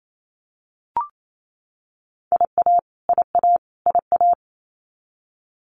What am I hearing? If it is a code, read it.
Morse code, heard as a keyed tone, about 28 wpm, 700 Hz. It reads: IA IA IA
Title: US States Repeated 3x 28wpm